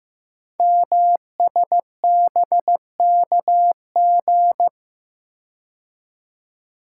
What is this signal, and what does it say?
Morse code, heard as a keyed tone, about 15 wpm, 700 Hz. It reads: MSBKG
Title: Code Groups - 5 Letters - 15wpm